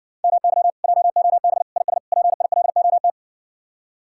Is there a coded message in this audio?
Yes: Morse code from 0.2 to 3.1 s, heard as a keyed tone, about 59 wpm, 700 Hz. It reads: MY JOB IS PILOT